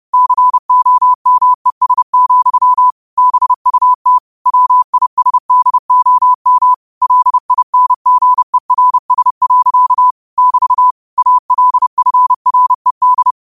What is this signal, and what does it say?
Morse code, heard as a keyed tone, about 30 wpm, 1000 Hz. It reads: COMES, BUT WISDOM LINGERS. <BT> ALFRED
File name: qotdfiles / 30wpm